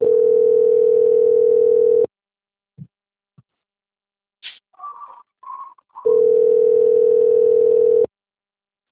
Ringback tone (North America), 440 + 480 Hz.